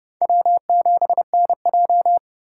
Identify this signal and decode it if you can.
Morse code, heard as a keyed tone, about 30 wpm, 700 Hz. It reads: W7NJ
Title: USA Call Signs - 4 Characters - 3x Character Spacing - 30wpm